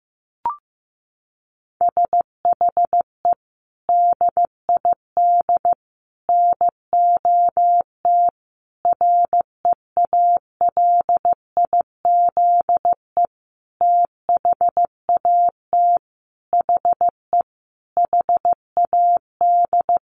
Morse code, heard as a keyed tone, about 15 wpm, 700 Hz. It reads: SHE DID NOT REALIZE THAT HE HAD
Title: Sentences from Top 1000 Words 15wpm